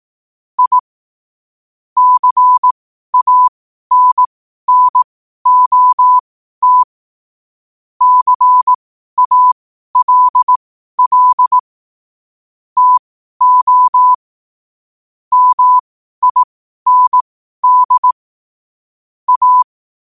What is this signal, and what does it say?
Morse code, heard as a keyed tone, about 18 wpm, spaced as for 12 wpm, 1000 Hz. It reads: I CANNOT CALL TO MIND A
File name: qotdfiles / 12wpm